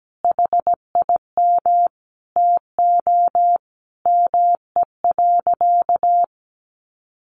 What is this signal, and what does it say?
Morse code, heard as a keyed tone, about 17 wpm, 700 Hz. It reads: HIM TO ME.
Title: Sentences from Top 1300 Words 17wpm